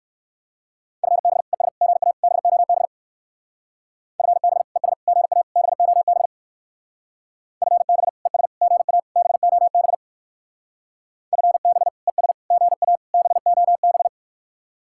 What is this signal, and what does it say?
Morse code, heard as a keyed tone, about 60 wpm, 700 Hz. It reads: FB ES GA BOB FB ES GA BOB FB ES GA BOB FB ES GA BOB